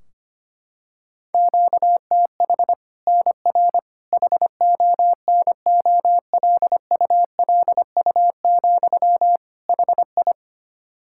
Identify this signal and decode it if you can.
Morse code, heard as a keyed tone, about 25 wpm, 700 Hz. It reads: QTH NR HONOLULU, HI
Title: Practice QSO - Exchange 1 - Beginning - 25wpm